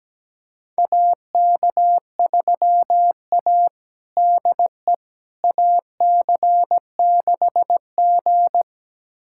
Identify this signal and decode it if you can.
Morse code, heard as a keyed tone, about 17 wpm, 700 Hz. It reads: AK3A DE AC6G
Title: USA Callsign DE Callsign - 4 Characters - 17wpm